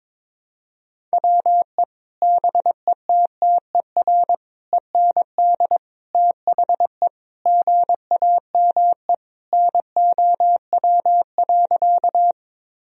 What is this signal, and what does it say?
Morse code, heard as a keyed tone, about 22 wpm, 700 Hz. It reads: WE BETTER END THE GAME NOW.
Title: Sentences from Top 1000 Words - Review Part 3 - 22wpm